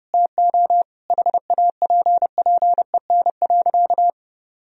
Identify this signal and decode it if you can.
Morse code, heard as a keyed tone, about 30 wpm, 700 Hz. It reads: TO HAPPEN.
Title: Sentences from Top 1000 Words - Review Part 4 - 30wpm